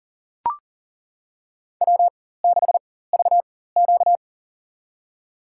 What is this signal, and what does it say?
Morse code, heard as a keyed tone, about 39 wpm, spaced as for 19 wpm, 700 Hz. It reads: W6VQ
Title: USA Call Signs - 4 Characters - 4x Character Spacing - 40wpm